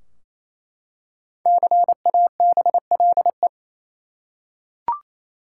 Morse code, heard as a keyed tone, about 28 wpm, 700 Hz. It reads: CABLE
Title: Five Letter Words - 28wpm